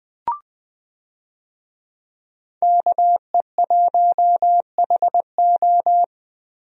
Morse code, heard as a keyed tone, about 20 wpm, 700 Hz. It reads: KE1HO
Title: USA Call Signs - 5 Characters - 20wpm